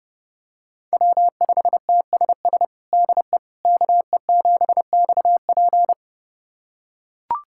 Morse code, heard as a keyed tone, about 32 wpm, 700 Hz. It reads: W5TSS DE KE7XP
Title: USA Callsign DE Callsign 30wpm